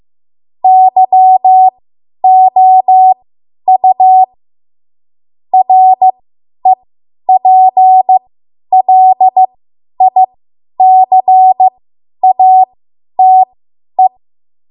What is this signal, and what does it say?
Morse code, heard as a keyed tone, about 15 wpm, spaced as for 10 wpm, 750 Hz. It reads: YOU REPLICATE